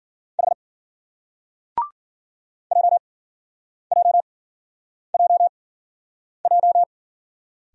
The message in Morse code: H JJJJ

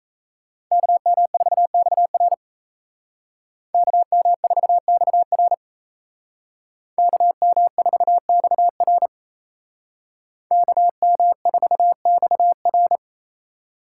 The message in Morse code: KM4XR KM4XR KM4XR KM4XR

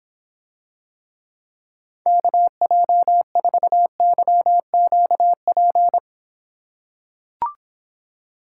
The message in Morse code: KJ4YQP

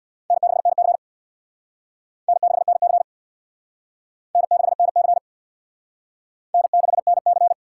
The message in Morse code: N6NC N6NC N6NC N6NC

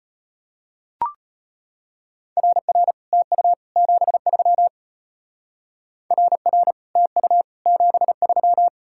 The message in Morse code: RR TU 73 RR TU 73

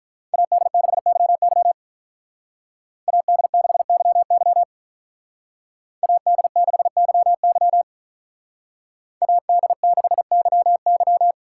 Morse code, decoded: AD6YY AD6YY AD6YY AD6YY

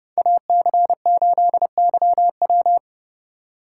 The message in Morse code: AC8YW

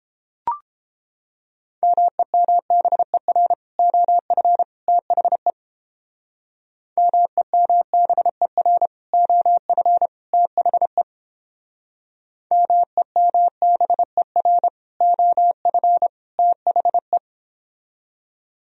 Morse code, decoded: MEMBER OF THE MEMBER OF THE MEMBER OF THE

